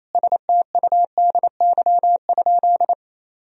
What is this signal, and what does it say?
Morse code, heard as a keyed tone, about 28 wpm, 700 Hz. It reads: STUDY?
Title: Sentences from Top 1000 Words - Review Part 1 - 28wpm